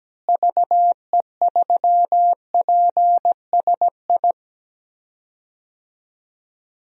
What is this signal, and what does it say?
Morse code, heard as a keyed tone, about 17 wpm, 700 Hz. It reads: VE3PSI
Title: DXCC-100 Call Signs - 6 Characters - Repeated 2x - 17wpm